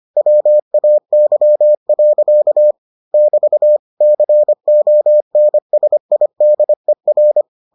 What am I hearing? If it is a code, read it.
Morse code, heard as a keyed tone, about 25 wpm, 600 Hz. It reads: WAY. <BT> CONSIDER